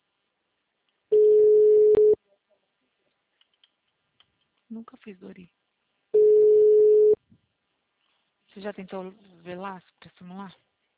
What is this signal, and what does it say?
Ringback tone (Europe, 425 Hz), with speech.
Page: Marilia - SP